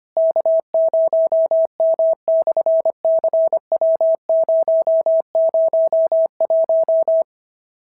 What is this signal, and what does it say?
Morse code, heard as a keyed tone, about 25 wpm, 650 Hz. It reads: K0M/CW001
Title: SOTA QSO Practice 25wpm